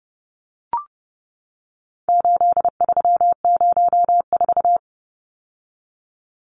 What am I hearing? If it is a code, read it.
Morse code, heard as a keyed tone, about 30 wpm, 700 Hz. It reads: 8304
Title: Numbers - 4-Digit 30wpm